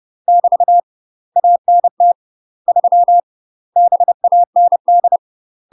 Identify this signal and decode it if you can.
Morse code, heard as a keyed tone, about 30 wpm, 700 Hz. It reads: <BT> ANT 3 BAND
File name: UK MASTER QSOs TRAINING / 30WPM / UK MASTER QSOs 0030